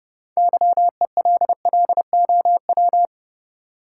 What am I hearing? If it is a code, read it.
Morse code, heard as a keyed tone, about 32 wpm, 700 Hz. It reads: YELLOW
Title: Six Letter Words - 30wpm